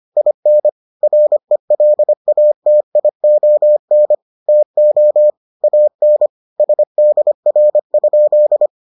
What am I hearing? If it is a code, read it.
Morse code, heard as a keyed tone, about 25 wpm, 600 Hz. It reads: IN RELATION TO AN SDR?